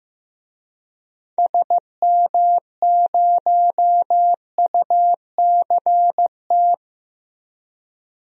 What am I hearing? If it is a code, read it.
Morse code, heard as a keyed tone, about 15 wpm, 700 Hz. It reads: SM0UCT
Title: DXCC-100 Call Signs - 6 Characters - Repeated 3x - 15wpm